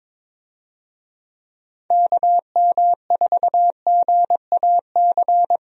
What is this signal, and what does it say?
Morse code, heard as a keyed tone, about 22 wpm, 700 Hz. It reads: KM4GAC
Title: USA Call Signs - 6 Characters - Repeated 3x - 22wpm